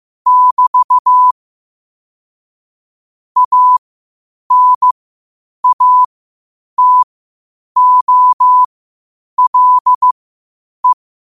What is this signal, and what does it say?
Morse code, heard as a keyed tone, about 15 wpm, spaced as for 8 wpm, 1000 Hz. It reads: <BT> ANATOLE
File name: qotdfiles / 8wpm